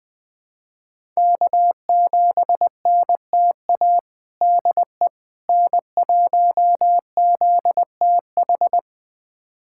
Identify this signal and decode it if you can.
Morse code, heard as a keyed tone, about 20 wpm, 700 Hz. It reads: K7NTA DE N1ZTH
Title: USA Callsign DE Callsign - 5 Characters - 20wpm